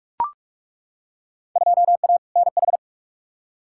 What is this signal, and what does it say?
Morse code, heard as a keyed tone, about 42 wpm, 700 Hz. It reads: 2A NH
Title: Field Day Exchanges - 45wpm